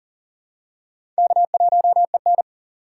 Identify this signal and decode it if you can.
Morse code, heard as a keyed tone, about 42 wpm, 700 Hz. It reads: K1EN
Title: USA Call Signs - 4 Characters - 2x Character Spacing - 40wpm